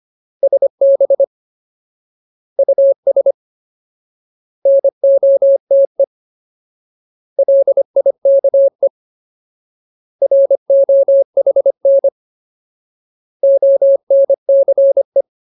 Morse code, heard as a keyed tone, about 25 wpm, 550 Hz. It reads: SB US NOTE LIKE ROHN ONCE
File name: Ultimate Word List / 25WPM Ultimate